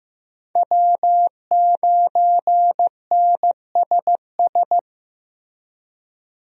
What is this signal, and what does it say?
Morse code, heard as a keyed tone, about 15 wpm, 700 Hz. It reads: W9NSS
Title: USA Call Signs - 5 Characters - Repeated 3x - 15wpm